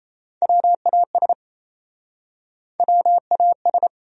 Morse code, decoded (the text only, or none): WAS WAS